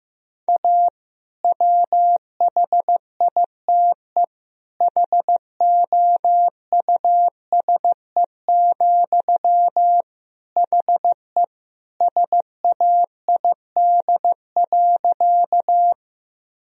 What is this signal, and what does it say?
Morse code, heard as a keyed tone, about 15 wpm, 700 Hz. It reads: A WHITE HOUSE, HE SAID.
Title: Sentences from Top 1000 Words - Review Part 1 - 15wpm